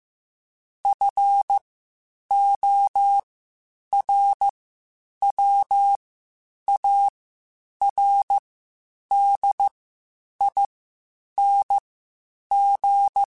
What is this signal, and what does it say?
Morse code, heard as a keyed tone, about 15 wpm, spaced as for 8 wpm, 800 Hz. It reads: FORWARDING
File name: Spaced / Chuckles Spaced-mp3s / Temptation sp @13